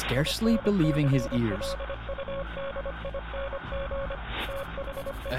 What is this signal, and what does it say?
Morse code, heard as a keyed tone, about 25 wpm, 550 Hz, with speech. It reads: INCLUDING THE